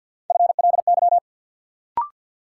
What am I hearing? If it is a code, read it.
Morse code, heard as a keyed tone, about 47 wpm, 700 Hz. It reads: FLY
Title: Sentences from Top 500 Words - New Word Drill - 50wpm